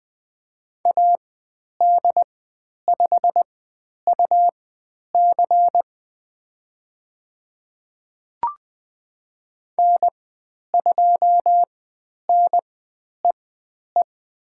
Morse code, heard as a keyed tone, about 20 wpm, spaced as for 10 wpm, 700 Hz. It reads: AD5UC N2NEE
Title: 102 Lesson 17g - Letter u - Call Signs 20wpm